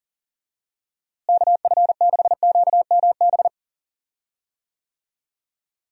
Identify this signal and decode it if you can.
Morse code, heard as a keyed tone, about 40 wpm, 700 Hz. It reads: KF6QMB
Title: USA Call Signs - 6 Characters - Repeated 3x - 40wpm